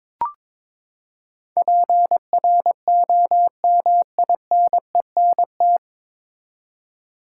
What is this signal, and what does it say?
Morse code, heard as a keyed tone, about 21 wpm, 700 Hz. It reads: PROMINENT
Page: Morse Code Ninja practice 22wpm